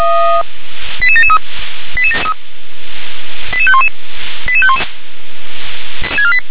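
Five-tone selective call (ZVEI), heard as quick runs of five tones, tones of about 70 ms. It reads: 88773 88511 88511
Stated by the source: ZVEI five-tone call received by SDR.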